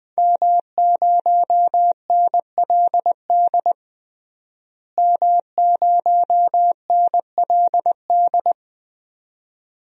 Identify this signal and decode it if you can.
Morse code, heard as a keyed tone, about 20 wpm, 700 Hz. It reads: M0NLD M0NLD
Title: DXCC-100 Call Signs - 5 Characters - Repeated 2x - 20wpm